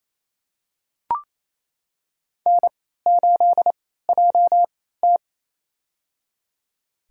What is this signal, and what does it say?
Morse code, heard as a keyed tone, about 28 wpm, spaced as for 16 wpm, 700 Hz. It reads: N8JT
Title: USA Call Signs - 3x Character Spacing - 28wpm